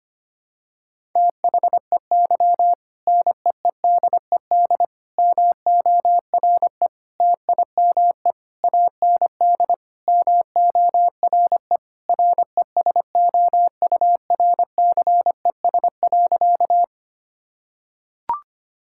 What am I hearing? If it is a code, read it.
Morse code, heard as a keyed tone, about 25 wpm, 700 Hz. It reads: THEY NEEDED MORE TIME AND MORE RESOURCES.
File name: practice / 25wpm / Sentences with 2-Word Phrases - 25wpm